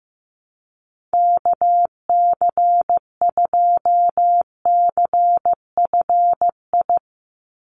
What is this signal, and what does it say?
Morse code, heard as a keyed tone, about 15 wpm, 700 Hz. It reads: KC2CFI
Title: USA Call Signs 15wpm